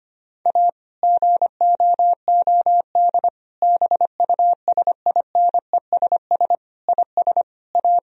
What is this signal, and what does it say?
Morse code, heard as a keyed tone, about 25 wpm, 700 Hz. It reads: A GOOD BUSINESS IS A
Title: Sentences from Top 1000 Words - Review Part 4 - 25wpm